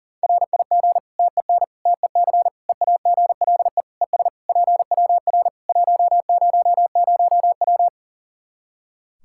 Morse code, heard as a keyed tone, about 40 wpm, 700 Hz. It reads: RIG TEN TEC EAGLE ES PWR 100W
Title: Practice QSO - Exchange 2 - Info - 40wpm